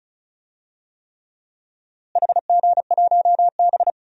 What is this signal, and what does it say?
Morse code, heard as a keyed tone, about 34 wpm, 700 Hz. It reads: HG1B
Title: DXCC-100 Call Signs - 4 Characters - Repeated 3x - 35wpm